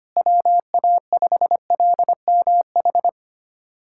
Morse code, heard as a keyed tone, about 25 wpm, 700 Hz. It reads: WA5LMH
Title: USA Callsign DE Callsign - 6 Characters - 25wpm